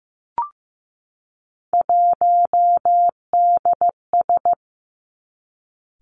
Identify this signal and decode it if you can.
Morse code, heard as a keyed tone, about 15 wpm, 700 Hz. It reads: 1DS
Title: Three Challenging Letter-Number Combinations 15wpm